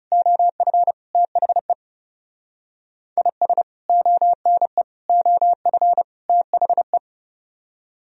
Morse code, heard as a keyed tone, about 35 wpm, 700 Hz. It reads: OF THE IS ONE OF THE
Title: Speed-Racing - Sets of 4 Words 30wpm